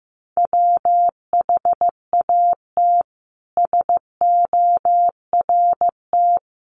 Morse code, heard as a keyed tone, about 15 wpm, 700 Hz. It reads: WHAT SORT
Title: Sentences from Top 1300 Words 15wpm